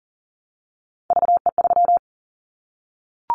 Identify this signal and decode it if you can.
Morse code, heard as a keyed tone, about 38 wpm, 700 Hz. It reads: VE3
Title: Callsign Prefixes for Top 25 DXCC - Repeated 3x - 40wpm